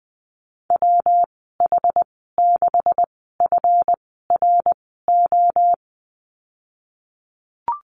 Morse code, heard as a keyed tone, about 20 wpm, spaced as for 15 wpm, 700 Hz. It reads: WH6FRO
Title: 173 Lesson 28g - Number 9 - Call Signs 20wpm